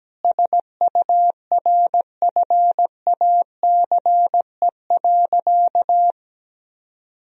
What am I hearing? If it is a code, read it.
Morse code, heard as a keyed tone, about 17 wpm, 700 Hz. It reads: SURFACE.
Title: Sentences from Top 1000 Words - Review Part 3 - 17wpm